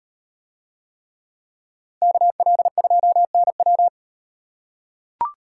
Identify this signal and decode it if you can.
Morse code, heard as a keyed tone, about 36 wpm, 700 Hz. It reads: KL2NW